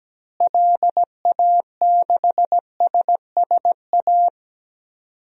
Morse code, heard as a keyed tone, about 17 wpm, 700 Hz. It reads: LA6SSA
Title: DXCC-100 Call Signs - 6 Characters - Repeated 2x - 17wpm